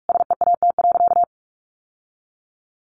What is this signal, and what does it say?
Morse code, heard as a keyed tone, about 44 wpm, 700 Hz. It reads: SEAT.
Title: Sentences from Top 1500 Words 45wpm